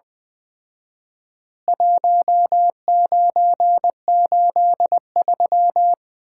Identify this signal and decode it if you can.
Morse code, heard as a keyed tone, about 20 wpm, 700 Hz. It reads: 1983
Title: Common QSO Phrases - 4x Word-Spacing - 20wpm